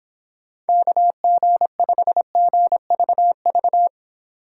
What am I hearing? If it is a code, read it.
Morse code, heard as a keyed tone, about 26 wpm, 700 Hz. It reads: KG5GVV